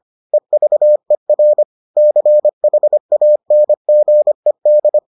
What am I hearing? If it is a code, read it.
Morse code, heard as a keyed tone, about 25 wpm, 600 Hz. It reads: EVER CHANGED